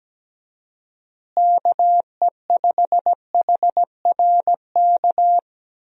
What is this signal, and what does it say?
Morse code, heard as a keyed tone, about 17 wpm, 700 Hz. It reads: KE5HRK
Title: USA Call Signs - 6 Characters - Repeated 2x - 17wpm